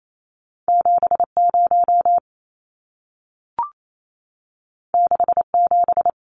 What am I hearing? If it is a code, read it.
Morse code, heard as a keyed tone, about 28 wpm, 700 Hz. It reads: 70 67
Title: Numbers - 2-Digit 28wpm